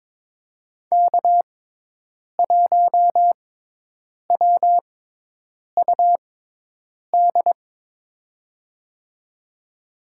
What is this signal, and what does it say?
Morse code, heard as a keyed tone, about 22 wpm, spaced as for 8 wpm, 700 Hz. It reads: K1WUD
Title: USA Call Signs - 5 Characters - 6x Character Spacing - 22wpm